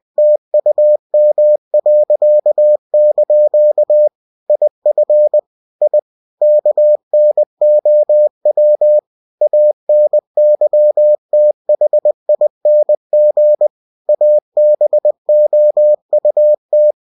Morse code, heard as a keyed tone, about 20 wpm, 600 Hz. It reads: TUM.) IF I KNOW ANYTHING ABOUT
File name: cw-books / winnie-the-pooh / winnie-20wpm-page-0013